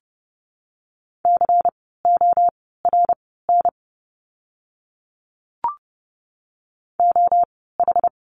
Morse code, heard as a keyed tone, about 30 wpm, spaced as for 17 wpm, 700 Hz. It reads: CORN OH
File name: Lesson 16f - Letter c - Words 30wpm